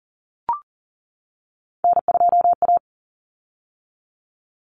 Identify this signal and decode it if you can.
Morse code, heard as a keyed tone, about 40 wpm, 700 Hz. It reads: N2A
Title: Common Cut Numbers - 3-Digit 40wpm